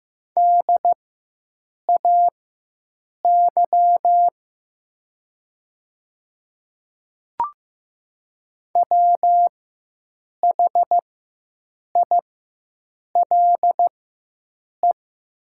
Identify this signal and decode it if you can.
Morse code, heard as a keyed tone, about 15 wpm, spaced as for 7 wpm, 700 Hz. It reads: DAY WHILE
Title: 4x Character Spacing - Top 300 Words - 15wpm